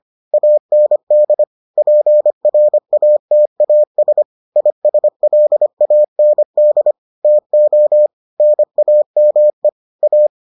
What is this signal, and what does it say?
Morse code, heard as a keyed tone, about 25 wpm, 600 Hz. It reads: AND PRATAS ISLAND TO NAME A